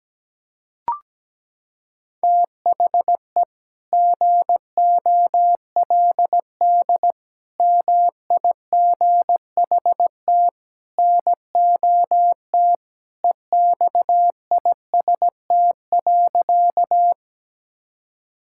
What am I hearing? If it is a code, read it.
Morse code, heard as a keyed tone, about 17 wpm, 700 Hz. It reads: THE GOLD MIGHT NOT EXIST.
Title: Sentences from Top 1000 Words - Review Part 5 - 17wpm